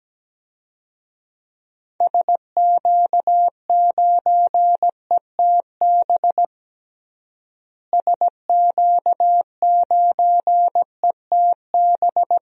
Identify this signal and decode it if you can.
Morse code, heard as a keyed tone, about 17 wpm, 700 Hz. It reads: SQ9ETB SQ9ETB
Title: DXCC-100 Call Signs - 6 Characters - Repeated 2x - 17wpm